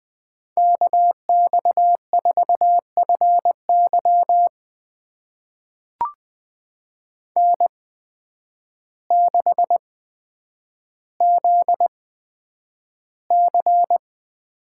Morse code, heard as a keyed tone, about 20 wpm, spaced as for 9 wpm, 700 Hz. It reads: KX4FY N6ZC
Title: USA Call Signs - 5 Characters - 8x Character Spacing - 20wpm